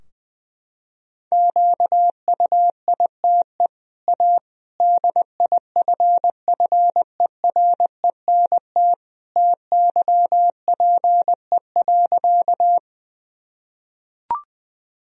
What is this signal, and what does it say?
Morse code, heard as a keyed tone, about 21 wpm, 700 Hz. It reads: QUITE A DIFFERENT TYPE.
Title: Sentences from Top 1000 Words - Review Part 3 - 20wpm